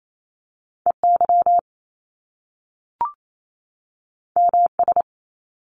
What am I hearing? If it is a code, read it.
Morse code, heard as a keyed tone, about 28 wpm, 700 Hz. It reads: E Y M S
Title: Top Two Letter Combinations in Words 28wpm